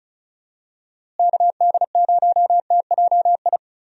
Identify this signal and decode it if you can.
Morse code, heard as a keyed tone, about 34 wpm, 700 Hz. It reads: KD0TJI
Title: USA Call Signs - 6 Characters - Repeated 3x - 35wpm